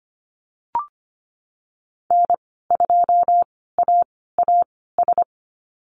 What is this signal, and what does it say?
Morse code, heard as a keyed tone, about 25 wpm, spaced as for 16 wpm, 700 Hz. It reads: N2AAS
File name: Lesson 15g - Number 5 - Call Signs 25wpm